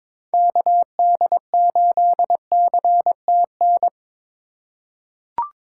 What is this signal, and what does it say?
Morse code, heard as a keyed tone, about 22 wpm, 700 Hz. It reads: KD8CTN
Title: USA Call Signs - 6 Characters - 2x Character Spacing - 22wpm